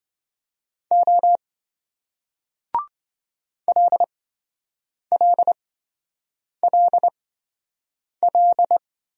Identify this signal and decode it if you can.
Morse code, heard as a keyed tone, about 27 wpm, spaced as for 7 wpm, 700 Hz. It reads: O LLLL